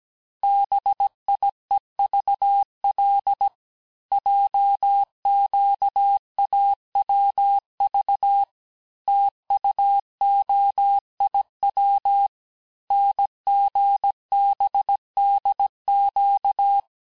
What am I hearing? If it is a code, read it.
Morse code, heard as a keyed tone, about 17 wpm, 800 Hz. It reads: BIEVL JQAWV TUOIW NGBDQ
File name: corrige-cours-1581-texte-chiffre-vitesse-16-mpm